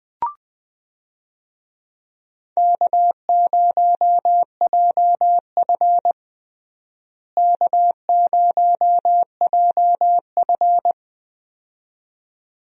Morse code, heard as a keyed tone, about 20 wpm, 700 Hz. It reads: K0JF K0JF
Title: USA Call Signs - 4 Characters - Repeated 2x - 20wpm